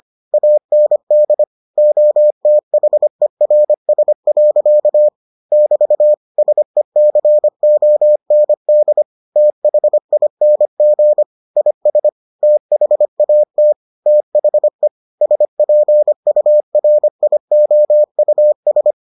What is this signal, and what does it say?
Morse code, heard as a keyed tone, about 25 wpm, 600 Hz. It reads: AND OTHERS. <BT> SECOND THING IS THAT THE SPURIOUS